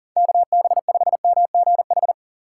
Morse code, heard as a keyed tone, about 37 wpm, 700 Hz. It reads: KB5MGH
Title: USA Call Signs - 6 Characters - 3x Character Spacing - 40wpm